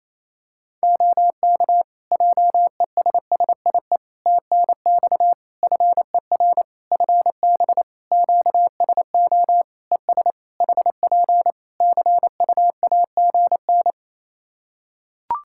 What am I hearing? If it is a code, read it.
Morse code, heard as a keyed tone, about 27 wpm, 700 Hz. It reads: OK JESSIE TNX FER FB QSO ES HP CUAGN
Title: Practice QSO - Exchange 4 - Ending - 28wpm